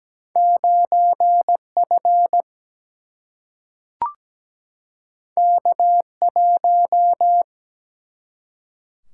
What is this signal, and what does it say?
Morse code, heard as a keyed tone, about 17 wpm, 700 Hz. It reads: 9 F K 1
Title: Single-Letter Single-Number 17wpm